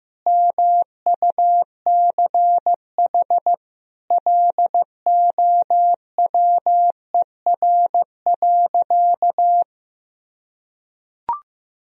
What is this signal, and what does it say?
Morse code, heard as a keyed tone, about 15 wpm, 700 Hz. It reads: MUCH LOWER.
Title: Sentences from Top 1000 Words - Review Part 4 - 15wpm